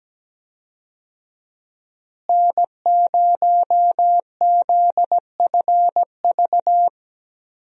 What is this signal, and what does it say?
Morse code, heard as a keyed tone, about 17 wpm, 700 Hz. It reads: N0ZFV
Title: USA Call Signs - 5 Characters - Repeated 2x - 17wpm